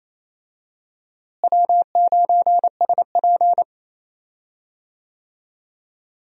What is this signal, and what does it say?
Morse code, heard as a keyed tone, about 28 wpm, 700 Hz. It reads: W9SP
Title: USA Call Signs - 4 Characters - Repeated 3x - 28wpm